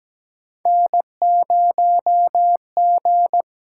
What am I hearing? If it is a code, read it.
Morse code, heard as a keyed tone, about 17 wpm, 700 Hz. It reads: N0G
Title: USA Call Signs - 8x Character Spacing - 17wpm